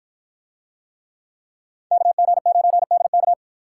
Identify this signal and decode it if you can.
Morse code, heard as a keyed tone, about 52 wpm, 700 Hz. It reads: KG9DK